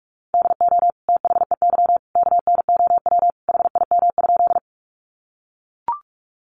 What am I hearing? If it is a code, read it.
Morse code, heard as a keyed tone, about 44 wpm, 700 Hz. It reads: DO THEY KNOW HIM?
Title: Sentences from Top 100 Words 45wpm